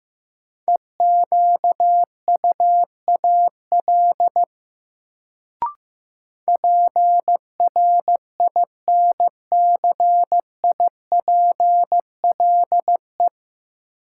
Morse code, heard as a keyed tone, about 15 wpm, 700 Hz. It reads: EQUAL PRINCIPLE
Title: Sentences from Top 1400 Words - New Word Drill Alt - 15wpm